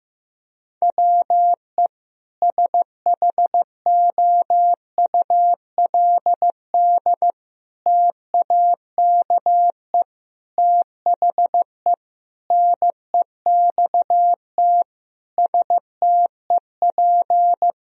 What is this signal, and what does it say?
Morse code, heard as a keyed tone, about 15 wpm, 700 Hz. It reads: WE SHOULD TAKE THE NEXT STEP
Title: Sentences from Top 1000 Words - Review Part 2 - 15wpm